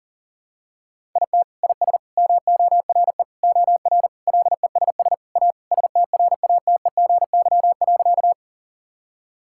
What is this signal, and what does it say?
Morse code, heard as a keyed tone, about 37 wpm, 700 Hz. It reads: IT IS MORE OR LESS A STRATEGY.
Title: Sentences from Top 1000 Words - Review Part 5 - 40wpm